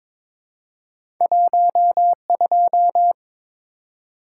Morse code, heard as a keyed tone, about 22 wpm, 700 Hz. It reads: 12
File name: Two Letter-Number Combinations - 22wpm